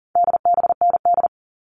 Morse code, heard as a keyed tone, about 40 wpm, 700 Hz. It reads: DBND
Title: All Cut Numbers - 4-Digit 40wpm